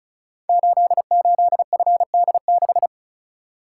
Morse code, heard as a keyed tone, about 36 wpm, 700 Hz. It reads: 88FD6
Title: Code Groups - 5 Letter:Numbers - 35wpm